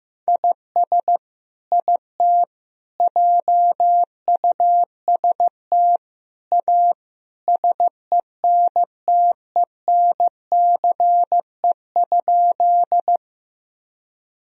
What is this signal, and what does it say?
Morse code, heard as a keyed tone, about 15 wpm, 700 Hz. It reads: IS IT JUST A SENTENCE?
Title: Sentences from Top 1000 Words - Review Part 1 - 15wpm